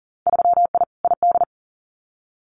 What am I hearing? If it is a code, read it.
Morse code, heard as a keyed tone, about 41 wpm, 700 Hz. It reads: 3I ID
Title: Winter Field Day Exchanges Encore 40wpm